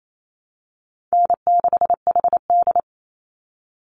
Morse code, heard as a keyed tone, about 28 wpm, 690 Hz. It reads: N6HD
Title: Call Signs Repeated 3x - 28wpm